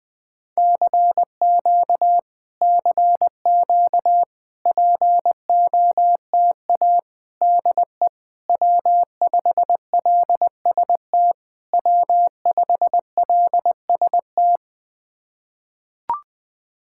Morse code, heard as a keyed tone, about 20 wpm, 700 Hz. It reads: CQ CQ POTA DE W5LST W5LST
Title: POTA QSO Practice - 20wpm